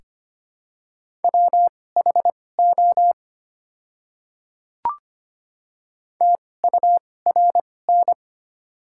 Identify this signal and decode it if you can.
Morse code, heard as a keyed tone, about 25 wpm, spaced as for 18 wpm, 700 Hz. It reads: WHO TURN
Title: 2x Character Spacing - Top 200 Words - 25wpm